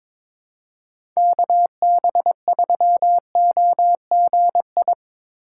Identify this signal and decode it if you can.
Morse code, heard as a keyed tone, about 22 wpm, 700 Hz. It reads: KB3OGI
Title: USA Call Signs - 6 Characters - 2x Character Spacing - 22wpm